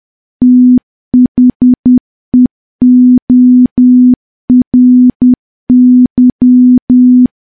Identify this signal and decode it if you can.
Morse code, heard as a keyed tone, about 10 wpm, 250 Hz. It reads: THEORY
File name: morse / morse-training-10WPM